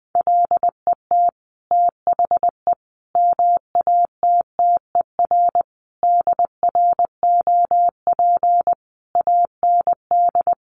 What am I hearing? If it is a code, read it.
Morse code, heard as a keyed tone, about 20 wpm, 700 Hz. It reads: LET THE MATTER DROP AND
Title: Sentences from Top 1400 Words 20wpm